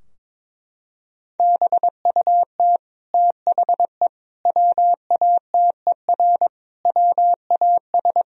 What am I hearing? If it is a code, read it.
Morse code, heard as a keyed tone, about 23 wpm, 700 Hz. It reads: BUT THE WATER WAS
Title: Sentences from Top 1000 Words - Review Part 3 - 22wpm